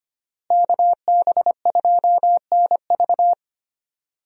Morse code, heard as a keyed tone, about 25 wpm, 700 Hz. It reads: KB2NV
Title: USA Call Signs - 5 Characters - Repeated 2x - 25wpm